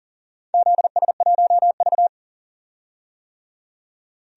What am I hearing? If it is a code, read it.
Morse code, heard as a keyed tone, about 40 wpm, 700 Hz. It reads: ZS1V
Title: DXCC-100 Call Signs - 4 Characters - 40wpm